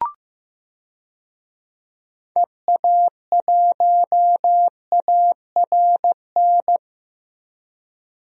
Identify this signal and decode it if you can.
Morse code, heard as a keyed tone, about 15 wpm, 700 Hz. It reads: EA1ARN
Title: DXCC-100 Call Signs - 6 Characters - 15wpm